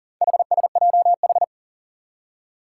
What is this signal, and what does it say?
Morse code, heard as a keyed tone, about 40 wpm, 700 Hz. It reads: HSJH
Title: Four Challenging Letter-Number Combinations 40wpm